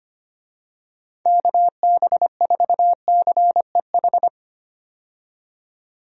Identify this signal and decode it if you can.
Morse code, heard as a keyed tone, about 25 wpm, 700 Hz. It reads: KB4CEH
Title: USA Call Signs - 6 Characters - 25wpm